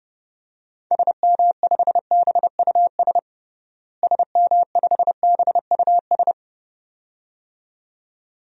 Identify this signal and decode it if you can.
Morse code, heard as a keyed tone, about 32 wpm, 700 Hz. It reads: SM5BUS SM5BUS
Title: DXCC-100 Call Signs - 6 Characters - Repeated 2x - 30wpm